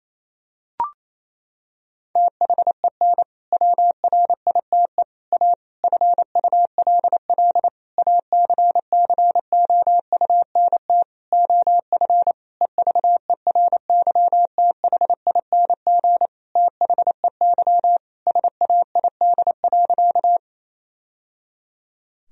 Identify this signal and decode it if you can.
Morse code, heard as a keyed tone, about 28 wpm, 700 Hz. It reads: THEN WRITE A FULL ACCOUNT OF EVERYTHING THEY SAID.
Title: Sentences from Top 1000 Words - Review Part 3 - 28wpm